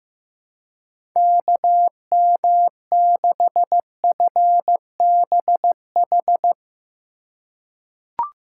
Morse code, heard as a keyed tone, about 15 wpm, 700 Hz. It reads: KM6FBH
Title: USA Call Signs - 6 Characters - 2x Character Spacing - 15wpm